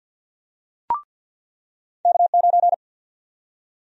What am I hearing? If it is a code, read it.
Morse code, heard as a keyed tone, about 49 wpm, 700 Hz. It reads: K9
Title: USA Call Sign Prefixes - 50wpm